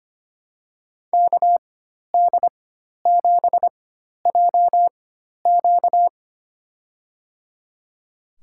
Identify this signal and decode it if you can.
Morse code, heard as a keyed tone, about 25 wpm, spaced as for 12 wpm, 700 Hz. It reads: KD7JQ
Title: USA Call Signs - 5 Characters - 4x Character Spacing - 25wpm